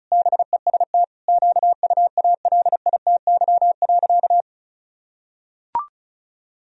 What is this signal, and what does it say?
Morse code, heard as a keyed tone, about 34 wpm, 700 Hz. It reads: BEST QUALITY.
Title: Sentences from Top 1000 Words - Review Part 4 - 35wpm